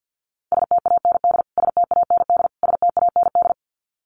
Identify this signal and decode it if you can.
Morse code, heard as a keyed tone, about 50 wpm, 700 Hz. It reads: STAND STAND STAND
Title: Top 300 Words Repeated 50wpm